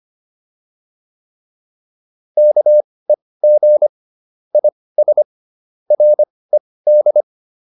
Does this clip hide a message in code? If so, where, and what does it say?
Yes: Morse code from 2.4 to 7.2 s, heard as a keyed tone, about 25 wpm, spaced as for 18 wpm, 600 Hz. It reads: KEG IS RED